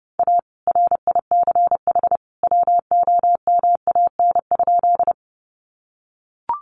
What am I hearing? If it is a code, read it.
Morse code, heard as a keyed tone, about 30 wpm, 700 Hz. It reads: A RICH WOMAN?
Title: Sentences from Top 1100 Words 30wpm